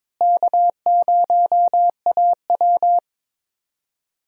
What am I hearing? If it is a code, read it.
Morse code, heard as a keyed tone, about 22 wpm, 700 Hz. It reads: K0AW
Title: USA Call Signs - 4 Characters - Repeated 3x - 22wpm